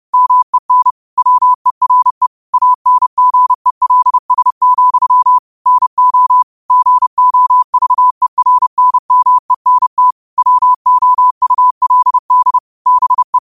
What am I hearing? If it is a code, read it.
Morse code, heard as a keyed tone, about 30 wpm, 1000 Hz. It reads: MEN WERE ANGELS, NO GOVERNMENT WOULD BE
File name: qotdfiles / 30wpm